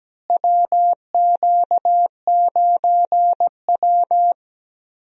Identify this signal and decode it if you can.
Morse code, heard as a keyed tone, about 17 wpm, 700 Hz. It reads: WQ9W
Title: USA Call Signs - 4 Characters - Repeated 2x - 17wpm